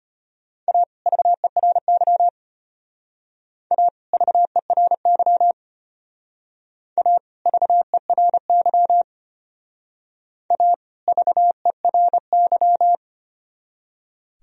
Morse code, heard as a keyed tone, about 32 wpm, 700 Hz. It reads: A VERY A VERY A VERY A VERY